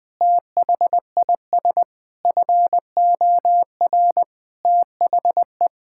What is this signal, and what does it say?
Morse code, heard as a keyed tone, about 20 wpm, 700 Hz. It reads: THIS FOR THE
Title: Sentences with 5-Word Phrases - Spoken Beginning - 20wpm